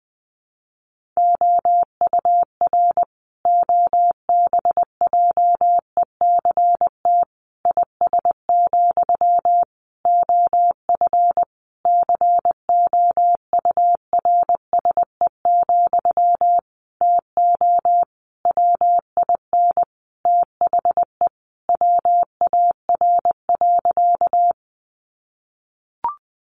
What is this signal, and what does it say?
Morse code, heard as a keyed tone, about 20 wpm, 700 Hz. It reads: OUR OBJECT IS, OF COURSE, TO WIN THE WAR.
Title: Sentences from Top 1500 Words 20wpm